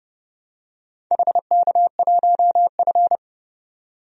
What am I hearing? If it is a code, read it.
Morse code, heard as a keyed tone, about 32 wpm, 700 Hz. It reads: HK1F
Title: DXCC-100 Call Signs - 4 Characters - 30wpm